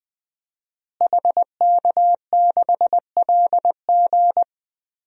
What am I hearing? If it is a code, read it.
Morse code, heard as a keyed tone, about 20 wpm, 700 Hz. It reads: HK6LG
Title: DXCC-100 Call Signs - 5 Characters - Repeated 3x - 20wpm